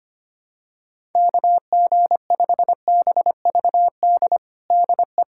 Morse code, heard as a keyed tone, about 25 wpm, 700 Hz. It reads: KG5BVD DE
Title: USA Callsign DE Callsign 25wpm